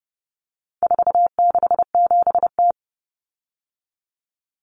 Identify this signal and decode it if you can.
Morse code, heard as a keyed tone, about 30 wpm, 700 Hz. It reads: 467T
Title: Common Cut Numbers - 4-Digit 30wpm